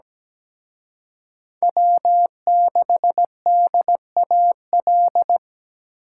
Morse code, heard as a keyed tone, about 20 wpm, 700 Hz. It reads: W6DAL
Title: Speed Racing - USA 1x3 Call Signs 17wpm